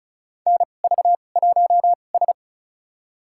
Morse code, heard as a keyed tone, about 36 wpm, spaced as for 25 wpm, 700 Hz. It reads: NV1S
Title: USA Call Signs - 4 Characters - 2x Character Spacing - 35wpm